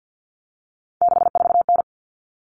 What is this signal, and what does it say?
Morse code, heard as a keyed tone, about 50 wpm, 700 Hz. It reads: 64N
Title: Common Cut Numbers - 3-Digit 50wpm